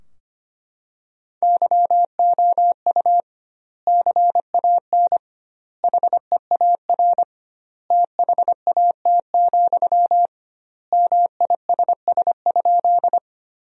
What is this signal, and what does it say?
Morse code, heard as a keyed tone, about 25 wpm, 700 Hz. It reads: YOU CAN HEAR THAT, MISS?
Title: 2x Word-Spacing - Sentences from Top 500 Words - 25wpm